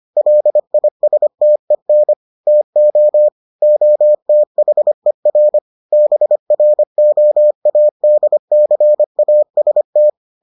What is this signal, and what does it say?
Morse code, heard as a keyed tone, about 25 wpm, 600 Hz. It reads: LISTEN TO OTHER BROADCAST